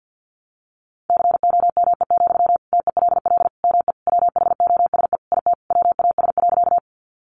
Morse code, heard as a keyed tone, about 50 wpm, 700 Hz. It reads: COME, TELL ME WHOSE IT WAS.
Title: Sentences from Top 1200 Words 50wpm